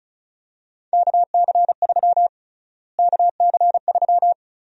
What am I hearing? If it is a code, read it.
Morse code, heard as a keyed tone, about 34 wpm, 700 Hz. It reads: KC3 KC3
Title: USA Call Sign Prefixes - Repeated 2x - 35wpm